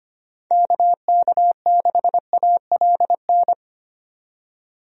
Morse code, heard as a keyed tone, about 25 wpm, 700 Hz. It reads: KK6ALN
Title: USA Call Signs - 6 Characters - 25wpm